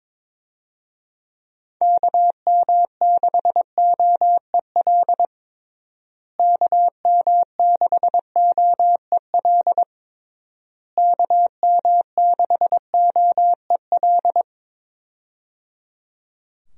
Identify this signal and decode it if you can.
Morse code, heard as a keyed tone, about 22 wpm, 700 Hz. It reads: KM6OEL KM6OEL KM6OEL
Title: USA Call Signs - 6 Characters - Repeated 3x - 22wpm